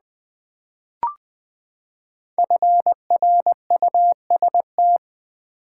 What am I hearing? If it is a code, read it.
Morse code, heard as a keyed tone, about 20 wpm, 700 Hz. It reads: FRUST
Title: Build A Word - 5 Letters At A Time - 10-Letter Words - 20wpm